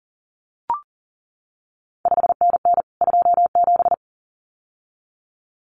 Morse code, heard as a keyed tone, about 38 wpm, 700 Hz. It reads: 5NN 27
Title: CQ WW DX Contest Exchange Encore - 40wpm